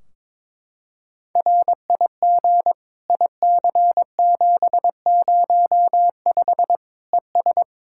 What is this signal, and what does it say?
Morse code, heard as a keyed tone, about 22 wpm, 700 Hz. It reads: RIG IC705 ES
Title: Practice QSO - Exchange 2 - Info - 22wpm